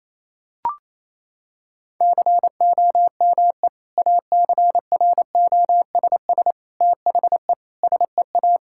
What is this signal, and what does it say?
Morse code, heard as a keyed tone, about 28 wpm, 700 Hz. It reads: COME ACROSS THE SEA
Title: Sentences from Top 1000 Words - Review Part 5 - 28wpm